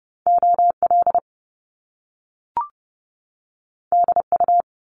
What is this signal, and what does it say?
Morse code, heard as a keyed tone, about 30 wpm, 700 Hz. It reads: OL DU
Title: Top Two Letter Combinations in Words 30wpm